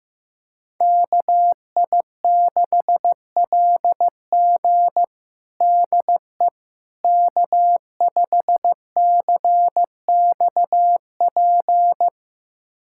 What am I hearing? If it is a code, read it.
Morse code, heard as a keyed tone, about 15 wpm, 700 Hz. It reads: KI6LG DE K5CXP
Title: USA Callsign DE Callsign - 5 Characters - 15wpm